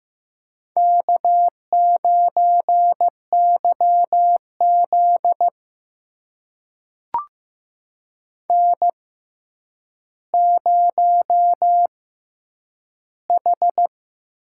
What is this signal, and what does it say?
Morse code, heard as a keyed tone, about 15 wpm, spaced as for 9 wpm, 700 Hz. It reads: K9YZ N0H
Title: USA Call Signs - 4 Characters - 6x Character Spacing - 15wpm